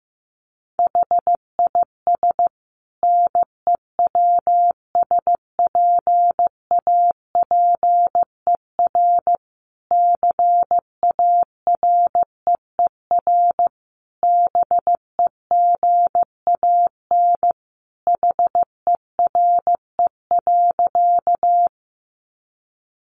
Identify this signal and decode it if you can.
Morse code, heard as a keyed tone, about 15 wpm, 700 Hz. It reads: HIS NEWSPAPER CAREER BEGAN HERE.
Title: Sentences from Top 1100 Words 15wpm